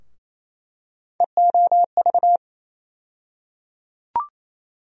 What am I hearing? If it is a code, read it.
Morse code, heard as a keyed tone, about 28 wpm, 700 Hz. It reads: EOV
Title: USA Call Sign Postfixes - 3 Letters 28wpm